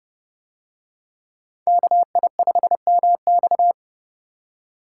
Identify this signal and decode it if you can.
Morse code, heard as a keyed tone, about 29 wpm, 700 Hz. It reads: KI5MX